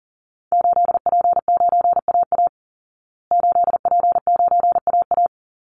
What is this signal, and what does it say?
Morse code, heard as a keyed tone, about 37 wpm, 700 Hz. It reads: 8P9AA 8P9AA
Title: Call Signs Repeated 3x - 40wpm